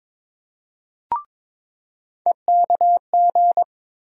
Morse code, heard as a keyed tone, about 22 wpm, 700 Hz. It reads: EKG
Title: USA Call Sign Postfixes - 3 Letters 22wpm